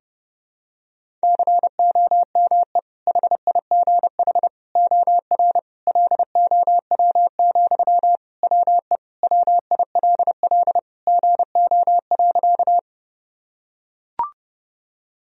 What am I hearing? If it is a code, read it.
Morse code, heard as a keyed tone, about 32 wpm, 700 Hz. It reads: COME HIGH OR LOW, WE WILL GO.
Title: Sentences from Top 1000 Words - Review Part 1 - 30wpm